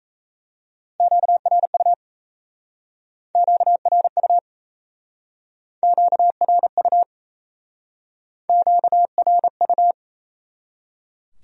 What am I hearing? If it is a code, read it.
Morse code, heard as a keyed tone, about 35 wpm, 700 Hz. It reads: QRU QRU QRU QRU